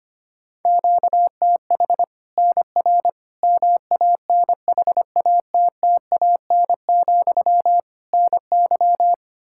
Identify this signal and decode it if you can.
Morse code, heard as a keyed tone, about 25 wpm, 700 Hz. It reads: QTH NR MANHATTAN, NY
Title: Practice QSO - Exchange 1 - Beginning - 25wpm